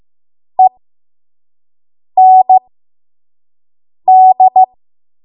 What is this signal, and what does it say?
Morse code, heard as a keyed tone, about 15 wpm, spaced as for 5 wpm, 750 Hz. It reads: END
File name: Morse / 05WPM